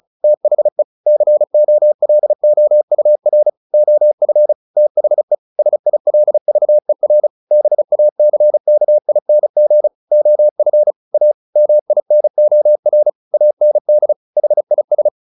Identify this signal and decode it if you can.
Morse code, heard as a keyed tone, about 34 wpm, 600 Hz. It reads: THE COLOUR OF THE SILVER BACKING OF A MINOR AND HIS
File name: the old man and the sea 2 35wpm